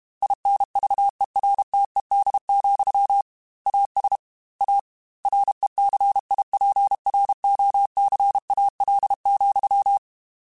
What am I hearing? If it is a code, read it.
Morse code, heard as a keyed tone, about 32 wpm, 800 Hz. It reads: INVERTED, AS A RECIPROCAL,
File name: Technical Subjects / Technical Subjects-mp3s / XC @30